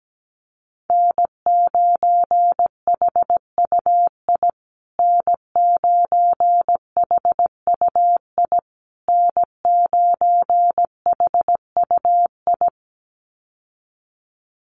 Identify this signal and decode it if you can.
Morse code, heard as a keyed tone, about 17 wpm, 700 Hz. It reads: N9HUI N9HUI N9HUI
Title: USA Call Signs Repeated 3x 17wpm